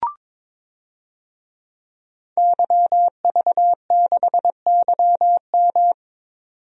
Morse code, heard as a keyed tone, about 22 wpm, 700 Hz. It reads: YV6YM
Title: DXCC-100 Call Signs - 5 Characters - 22wpm